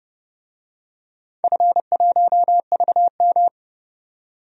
Morse code, heard as a keyed tone, about 30 wpm, 700 Hz. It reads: F1VM
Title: DXCC-100 Call Signs - 4 Characters - 30wpm